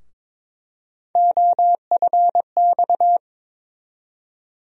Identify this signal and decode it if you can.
Morse code, heard as a keyed tone, about 22 wpm, 700 Hz. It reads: OFX
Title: USA Call Sign Postfixes - 3 Letters 22wpm